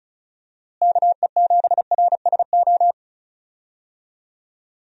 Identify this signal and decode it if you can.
Morse code, heard as a keyed tone, about 34 wpm, 700 Hz. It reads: KE7RSO
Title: USA Call Signs - 6 Characters - Repeated 3x - 35wpm